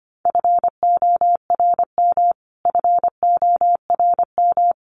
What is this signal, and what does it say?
Morse code, heard as a keyed tone, about 25 wpm, 700 Hz. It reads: FORM FORM
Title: Top 300 Words Repeated 25wpm